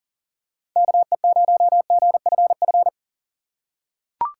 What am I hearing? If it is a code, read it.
Morse code, heard as a keyed tone, about 40 wpm, 690 Hz. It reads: KE0GFF
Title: USA Call Signs - 4x Character Spacing - 40wpm